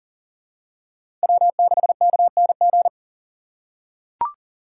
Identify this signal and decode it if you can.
Morse code, heard as a keyed tone, about 40 wpm, 700 Hz. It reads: W6KNG
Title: USA Call Signs - 5 Characters - 6x Character Spacing - 40wpm